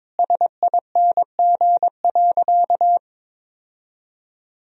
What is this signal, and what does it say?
Morse code, heard as a keyed tone, about 21 wpm, 700 Hz. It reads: SING.
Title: Sentences from Top 1000 Words - Review Part 2 - 22wpm